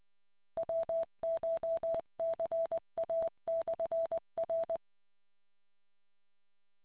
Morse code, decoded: WCHCA/R